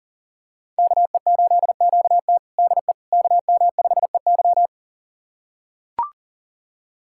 Morse code, heard as a keyed tone, about 40 wpm, 690 Hz. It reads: KE8QT DE KM5EY
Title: USA Callsign DE Callsign - 5 Characters - 40wpm